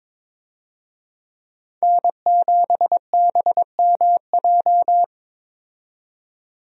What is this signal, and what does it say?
Morse code, heard as a keyed tone, about 22 wpm, 700 Hz. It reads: N7BMJ
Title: USA Call Signs - 5 Characters - 22wpm